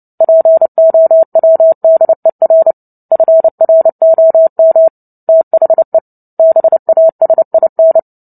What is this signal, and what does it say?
Morse code, heard as a keyed tone, about 30 wpm, 650 Hz. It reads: POWDER FROM THE BASIN
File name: Book Two / 03-Chapter 3 - 29 WPM